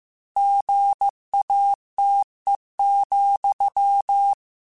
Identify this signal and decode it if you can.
Morse code, heard as a keyed tone, about 15 wpm, 800 Hz. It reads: GATE,
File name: Technical Subjects / Technical Subjects-mp3s / OR Gate @13